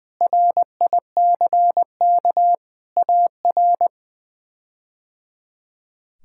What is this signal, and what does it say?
Morse code, heard as a keyed tone, about 20 wpm, 700 Hz. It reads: RICK AR
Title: K1USN SST QSO Practice - 20wpm